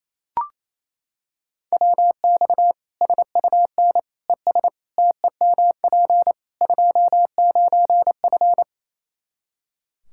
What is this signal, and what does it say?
Morse code, heard as a keyed tone, about 28 wpm, 700 Hz. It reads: WX SUN ES TEMP 29F
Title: Practice QSO - Exchange 2 - Info - 28wpm